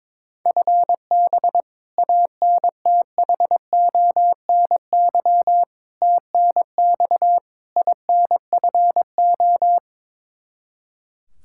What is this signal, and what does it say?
Morse code, heard as a keyed tone, about 22 wpm, 700 Hz. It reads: FB ANTHONY TNX INFO
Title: K1USN SST QSO Practice 22wpm